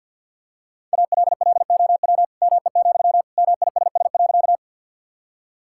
Morse code, heard as a keyed tone, about 50 wpm, 700 Hz. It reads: ALLOW ME, MISS.
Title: Sentences from Top 1000 Words - Review Part 5 - 50wpm